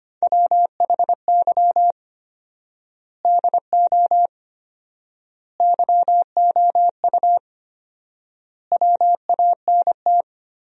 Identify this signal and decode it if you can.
Morse code, heard as a keyed tone, about 25 wpm, 700 Hz. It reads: WHY DO YOU WANT
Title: 4x Word-Spacing - Sentences from Top 200 Words - 25wpm